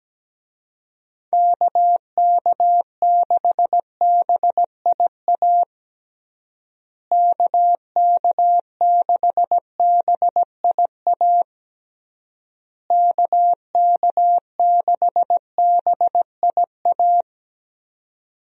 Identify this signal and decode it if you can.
Morse code, heard as a keyed tone, about 17 wpm, 700 Hz. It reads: KK6BIA KK6BIA KK6BIA
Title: USA Call Signs - 6 Characters - Repeated 3x - 17wpm